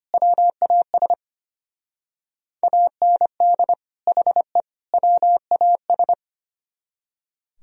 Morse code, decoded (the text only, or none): WAS AND HE WAS